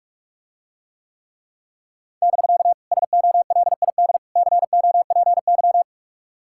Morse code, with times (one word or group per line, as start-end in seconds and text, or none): 2.2-2.7 <BK>
2.9-4.2 SOLID
4.4-5.8 COPY